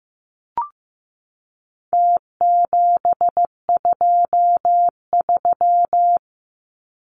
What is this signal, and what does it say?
Morse code, heard as a keyed tone, about 15 wpm, 700 Hz. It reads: T723
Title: Numbers - Contest Style - 4-Digit 15wpm